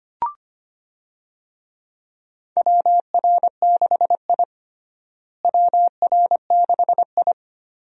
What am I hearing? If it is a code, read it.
Morse code, heard as a keyed tone, about 25 wpm, 700 Hz. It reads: WR6I WR6I
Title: USA Call Signs - 4 Characters - Repeated 2x - 25wpm